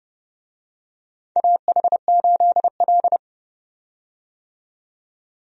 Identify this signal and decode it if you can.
Morse code, heard as a keyed tone, about 30 wpm, 700 Hz. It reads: AH8L
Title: USA Call Signs - 4 Characters - 30wpm